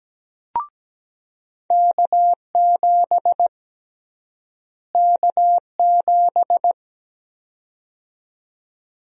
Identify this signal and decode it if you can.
Morse code, heard as a keyed tone, about 17 wpm, 700 Hz. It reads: K7 K7
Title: USA Call Sign Prefixes - Repeated 2x - 17wpm